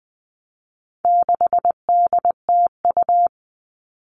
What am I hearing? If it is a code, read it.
Morse code, heard as a keyed tone, about 20 wpm, 700 Hz. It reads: 6DTU
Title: All Cut Numbers - 4-Digit 20wpm